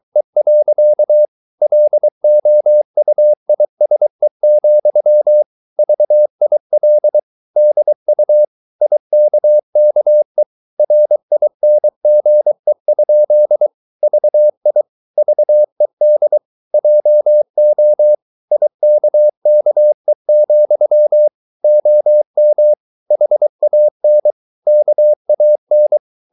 Morse code, heard as a keyed tone, about 23 wpm, 600 Hz. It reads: E. LOUISE, VIL DU IKKE RINGE? VI VED JO IKKE, OM HAN KAN
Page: Ferie 23 wpm